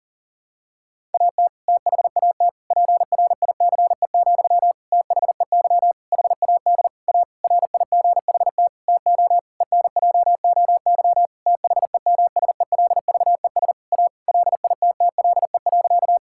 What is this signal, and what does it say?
Morse code, heard as a keyed tone, about 40 wpm, 700 Hz. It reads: AT THAT PRICE, THEY HAD A RIGHT TO ENJOY THEMSELVES A LITTLE.
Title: Sentences from Top 1000 Words - Review Part 5 - 40wpm